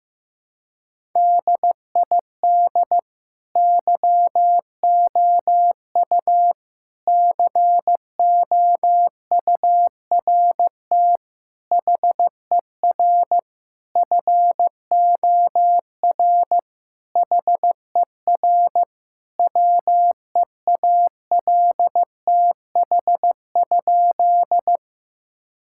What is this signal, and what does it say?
Morse code, heard as a keyed tone, about 16 wpm, 700 Hz. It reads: DID YOU COURT HER FOR HER WEALTH?
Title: Sentences from Top 1600 Words 15wpm